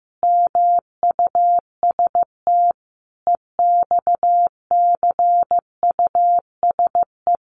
Morse code, heard as a keyed tone, about 15 wpm, 700 Hz. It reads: MUST EXCUSE
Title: Sentences from Top 700 Words 15wpm